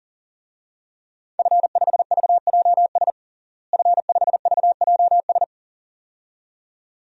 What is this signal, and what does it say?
Morse code, heard as a keyed tone, about 37 wpm, 700 Hz. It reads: F5VJS F5VJS
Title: DXCC-100 Call Signs - 5 Characters - Repeated 2x - 40wpm